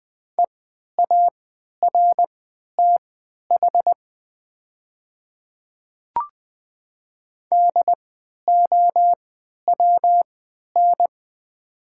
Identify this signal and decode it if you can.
Morse code, heard as a keyed tone, about 20 wpm, spaced as for 11 wpm, 700 Hz. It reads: EARTH DOWN
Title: 3x Character Spacing - Top 200 Words - 20wpm